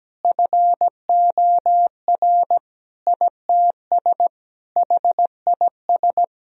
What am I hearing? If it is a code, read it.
Morse code, heard as a keyed tone, about 17 wpm, 700 Hz. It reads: FOR ITS HIS
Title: Sentences from Top 1000 Words - Review Part 2 - 17wpm